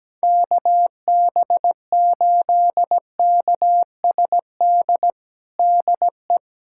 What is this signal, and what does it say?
Morse code, heard as a keyed tone, about 17 wpm, 700 Hz. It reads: KB8KSD DE
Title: USA Callsign DE Callsign 17wpm